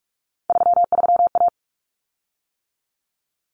Morse code, heard as a keyed tone, about 44 wpm, 700 Hz. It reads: 33A
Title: Common Cut Numbers - 3-Digit 45wpm